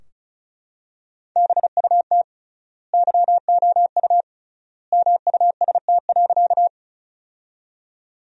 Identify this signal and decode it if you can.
Morse code, heard as a keyed tone, about 34 wpm, 700 Hz. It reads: BUT YOU MUST.
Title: 3x Word-Spacing - Sentences from Top 200 Words - 35wpm